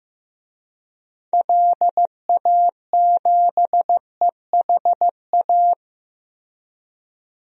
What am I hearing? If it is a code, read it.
Morse code, heard as a keyed tone, about 15 wpm, 700 Hz. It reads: LA7EHA
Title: DXCC-100 Call Signs - 6 Characters - Repeated 3x - 15wpm